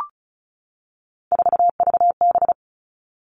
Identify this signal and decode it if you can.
Morse code, heard as a keyed tone, about 36 wpm, 700 Hz. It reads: 4VB
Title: Three Challenging Letter-Number Combinations 35wpm